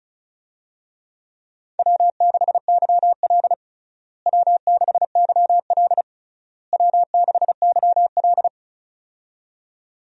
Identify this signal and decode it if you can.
Morse code, heard as a keyed tone, about 36 wpm, 700 Hz. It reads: W6YL W6YL W6YL
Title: USA Call Signs - 4 Characters - Repeated 3x - 35wpm